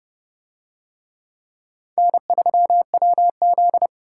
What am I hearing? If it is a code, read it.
Morse code, heard as a keyed tone, about 28 wpm, 700 Hz. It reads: N3WZ